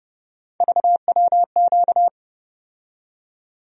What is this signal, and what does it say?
Morse code, heard as a keyed tone, about 32 wpm, 700 Hz. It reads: VWQ
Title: USA Call Sign Postfixes - 3 Letters 30wpm